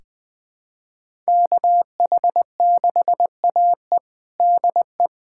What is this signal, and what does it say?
Morse code, heard as a keyed tone, about 20 wpm, 700 Hz. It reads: KH6AE DE
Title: USA Callsign DE Callsign - 5 Characters - 20wpm